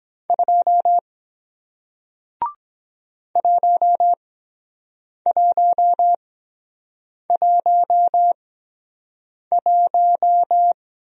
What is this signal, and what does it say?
Morse code, heard as a keyed tone, about 23 wpm, spaced as for 7 wpm, 700 Hz. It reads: 2 1111